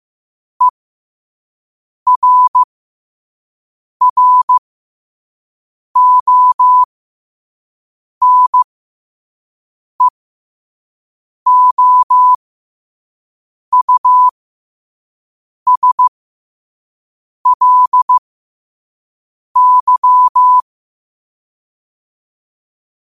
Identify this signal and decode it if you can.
Morse code, heard as a keyed tone, about 15 wpm, spaced as for 5 wpm, 1000 Hz. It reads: ERRONEOUSLY